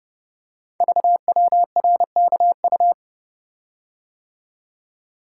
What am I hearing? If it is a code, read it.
Morse code, heard as a keyed tone, about 28 wpm, 700 Hz. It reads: VWRKU
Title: Code Groups - 5 Letter:Numbers - 30wpm